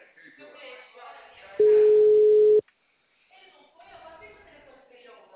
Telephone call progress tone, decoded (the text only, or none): ringback tone